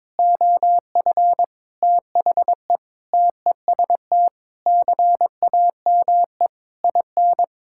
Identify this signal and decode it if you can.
Morse code, heard as a keyed tone, about 22 wpm, 700 Hz. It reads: OF THE TEST CAME IN
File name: practice / 22wpm / Sentences with 4-Word Phrases - 22wpm